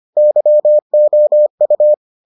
Morse code, heard as a keyed tone, about 25 wpm, 600 Hz. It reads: YOU